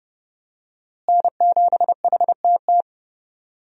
Morse code, heard as a keyed tone, about 30 wpm, 700 Hz. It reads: N7HTT
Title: USA Call Signs - 5 Characters - Repeated 2x - 30wpm